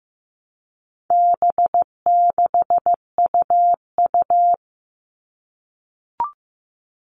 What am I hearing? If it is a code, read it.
Morse code, heard as a keyed tone, about 15 wpm, 700 Hz. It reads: B6UU
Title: All Cut Numbers - 4-Digit 15wpm